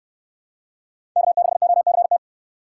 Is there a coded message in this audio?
Yes: Morse code from 1.2 to 2.2 s, heard as a keyed tone, about 69 wpm, 700 Hz. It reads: K6OCT